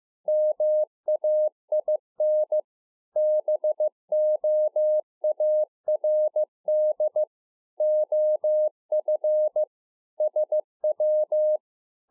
Morse code, heard as a keyed tone, about 15 wpm, 610 Hz. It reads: MAIN BOARD OF SW
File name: 2011-04-mp3 / A SMALL TRANCEIVER 2 15WPM